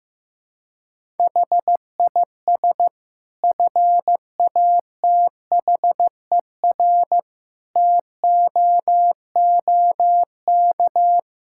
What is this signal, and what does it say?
Morse code, heard as a keyed tone, about 14 wpm, 700 Hz. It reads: HIS FATHER TOOK
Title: Sentences from Top 1000 Words - Review Part 5 - 15wpm